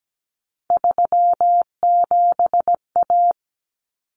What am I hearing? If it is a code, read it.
Morse code, heard as a keyed tone, about 17 wpm, 690 Hz. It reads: 37A
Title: Common Cut Numbers - 3-Digit 17wpm